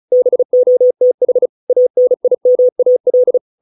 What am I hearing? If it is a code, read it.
Morse code, heard as a keyed tone, about 34 wpm, 500 Hz. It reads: BOTH ANIMAL